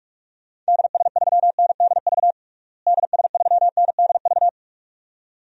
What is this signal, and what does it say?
Morse code, heard as a keyed tone, about 41 wpm, 700 Hz. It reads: DS3NDV DS3NDV
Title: DXCC-100 Call Signs - 6 Characters - Repeated 2x - 45wpm